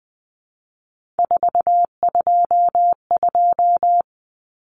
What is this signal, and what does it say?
Morse code, heard as a keyed tone, about 20 wpm, 700 Hz. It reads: 422
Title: Numbers - 3-Digit 20wpm